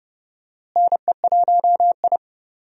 Morse code, heard as a keyed tone, about 30 wpm, 700 Hz. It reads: NE1I
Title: USA Call Signs - 4 Characters - Repeated 2x - 30wpm